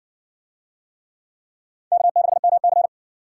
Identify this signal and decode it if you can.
Morse code, heard as a keyed tone, about 57 wpm, 690 Hz. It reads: K6MC